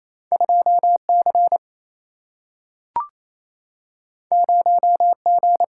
Morse code, heard as a keyed tone, about 28 wpm, 700 Hz. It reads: 2C 0G
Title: Single-Letter Single-Number 28wpm